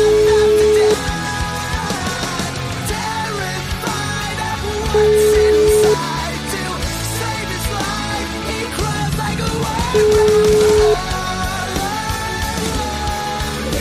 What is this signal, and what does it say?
Ringback tone (Europe, 425 Hz), with noise.